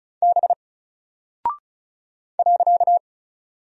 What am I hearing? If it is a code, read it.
Morse code, heard as a keyed tone, about 35 wpm, 700 Hz. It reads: B .
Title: Single Character (Letter:Number:Punctuation:Prosign) 35wpm